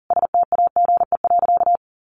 Morse code, heard as a keyed tone, about 40 wpm, 700 Hz. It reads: STAGE.
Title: Sentences from Top 1200 Words 40wpm